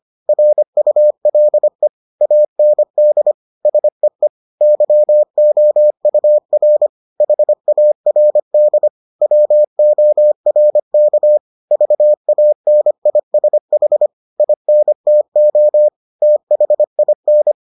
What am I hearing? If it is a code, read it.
Morse code, heard as a keyed tone, about 25 wpm, 600 Hz. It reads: RULE AND SEE YOUR HARD WORK VANISH INTO THIN